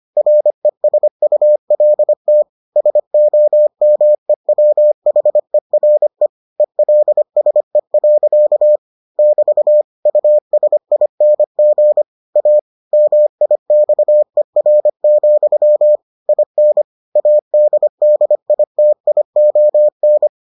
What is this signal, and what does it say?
Morse code, heard as a keyed tone, about 25 wpm, 600 Hz. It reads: RESULT SOMEWHERE ELSE. <BT> USING A MIXER, IN ADDITION